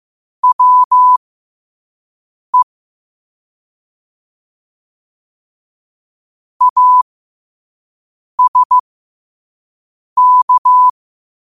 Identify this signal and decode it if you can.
Morse code, heard as a keyed tone, about 15 wpm, spaced as for 5 wpm, 1000 Hz. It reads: WE ASK
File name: qotdfiles / 5wpm